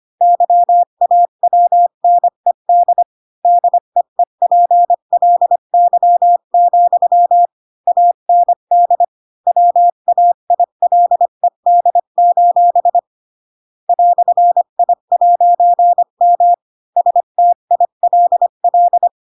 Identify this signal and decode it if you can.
Morse code, heard as a keyed tone, about 25 wpm, 700 Hz. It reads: YAWNED DEEPLY, AND WAILED: "I'M STILL
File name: Pinocchio / Pinocchio ch7 20 25 ik1hge